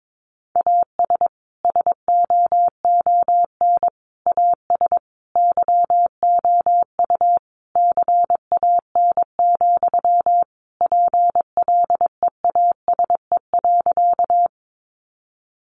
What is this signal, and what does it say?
Morse code, heard as a keyed tone, about 22 wpm, 700 Hz. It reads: AS SOON AS YOU CAN, PLEASE.
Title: Sentences from Top 1500 Words 22wpm